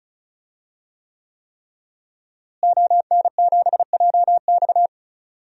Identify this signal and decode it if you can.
Morse code, heard as a keyed tone, about 34 wpm, 700 Hz. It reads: ON7JX
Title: DXCC-100 Call Signs - 5 Characters - Repeated 3x - 35wpm